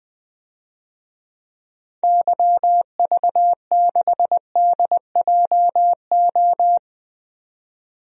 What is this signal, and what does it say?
Morse code, heard as a keyed tone, about 20 wpm, 700 Hz. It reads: YV6DJO
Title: DXCC-100 Call Signs - 6 Characters - 20wpm